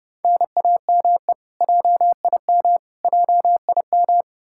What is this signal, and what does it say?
Morse code, heard as a keyed tone, about 30 wpm, 700 Hz. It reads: NAME JIM JIM
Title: Practice QSO - Exchange 1 - Beginning - 30wpm